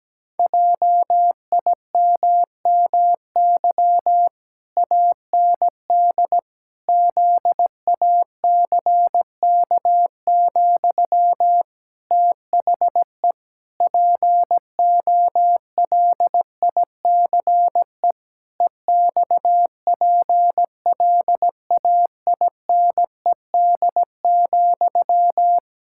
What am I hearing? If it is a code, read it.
Morse code, heard as a keyed tone, about 17 wpm, 700 Hz. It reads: JIMMY AND ZACK, THE POLICE EXPLAINED,
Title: Pangram Challenge v2 - 17wpm